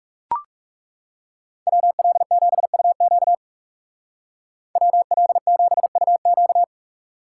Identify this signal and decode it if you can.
Morse code, heard as a keyed tone, about 35 wpm, 700 Hz. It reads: WL7UQ WL7UQ
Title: Speed Racing - USA 2x2 Call Signs 30wpm